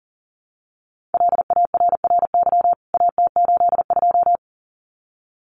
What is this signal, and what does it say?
Morse code, heard as a keyed tone, about 40 wpm, 700 Hz. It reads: LARRY AT82
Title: CWT Exchange 40wpm